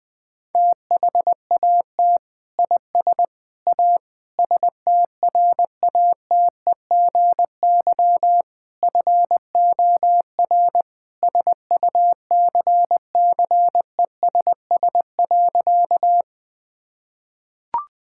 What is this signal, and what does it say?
Morse code, heard as a keyed tone, about 20 wpm, 700 Hz. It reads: THAT IS A STRATEGY FOR SUCCESS.
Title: Sentences from Top 1000 Words - Review Part 5 - 20wpm